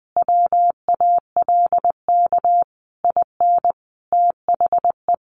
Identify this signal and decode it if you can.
Morse code, heard as a keyed tone, about 20 wpm, 700 Hz. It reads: WALK IN THE
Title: Sentences from Top 1400 Words 20wpm